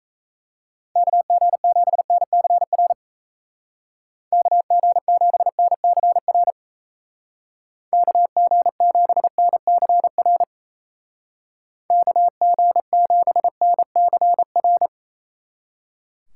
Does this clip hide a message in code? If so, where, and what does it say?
Yes: Morse code from 1.0 to 14.9 s, heard as a keyed tone, about 35 wpm, 700 Hz. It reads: KG7NCR KG7NCR KG7NCR KG7NCR